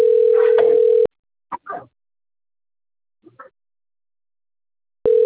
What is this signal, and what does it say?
Ringback tone (North America), 440 + 480 Hz.
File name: out- - - - - .83359